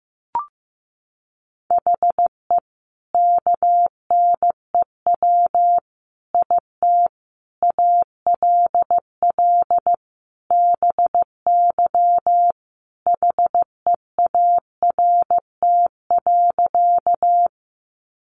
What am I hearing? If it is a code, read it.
Morse code, heard as a keyed tone, about 14 wpm, 700 Hz. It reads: HE KNEW IT ALL BY HEART.
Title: Sentences from Top 1200 Words 15wpm